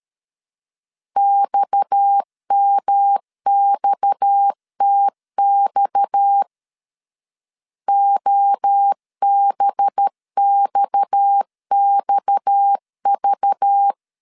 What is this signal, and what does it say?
Morse code, heard as a keyed tone, about 13 wpm, 800 Hz. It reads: XMXTX OBXXV
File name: LEZIONI / 6 PKX0 / 6 PKX 12wpm 2024